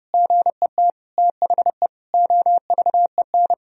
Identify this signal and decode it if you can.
Morse code, heard as a keyed tone, about 30 wpm, 700 Hz. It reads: GET THE OVEN
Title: Sentences from Top 1000 Words - Review Part 4 - 30wpm